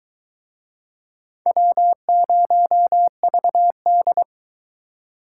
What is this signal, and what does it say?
Morse code, heard as a keyed tone, about 23 wpm, 700 Hz. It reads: W0VD